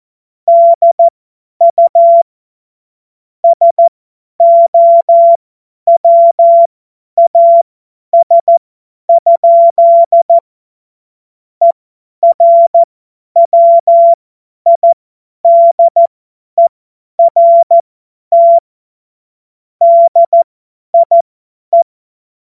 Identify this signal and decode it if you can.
Morse code, heard as a keyed tone, about 14 wpm, spaced as for 10 wpm, 680 Hz. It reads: DU SOWAS? ERWIDERT DIE
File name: Morsekurs / Witze / Witz 19. Oktober 2021, 50 F70